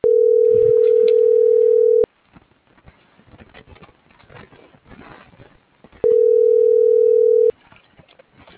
Ringback tone (North America), 440 + 480 Hz.
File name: CENTRAL GENERAL / out america.mejia